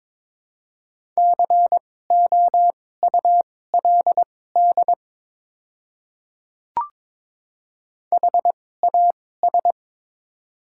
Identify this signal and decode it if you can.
Morse code, heard as a keyed tone, about 22 wpm, spaced as for 16 wpm, 700 Hz. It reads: COULD HAS
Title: 2x Character Spacing - Top 200 Words - 22wpm